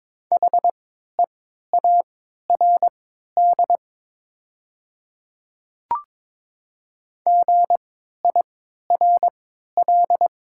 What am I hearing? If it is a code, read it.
Morse code, heard as a keyed tone, about 22 wpm, spaced as for 13 wpm, 700 Hz. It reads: HEARD GIRL
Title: 3x Character Spacing - Top 500 Words - 22wpm